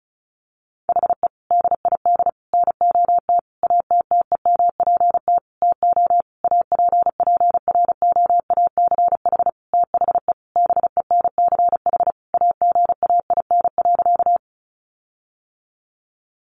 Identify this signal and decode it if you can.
Morse code, heard as a keyed tone, about 34 wpm, 700 Hz. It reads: HE DID NOT ATTEMPT TO APPROACH THE BENCH AGAIN.
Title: Sentences from Top 1400 Words 35wpm